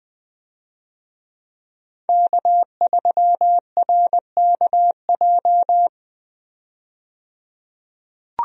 Morse code, heard as a keyed tone, about 20 wpm, 700 Hz. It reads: K3RKJ
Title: USA Call Signs - 5 Characters - 20wpm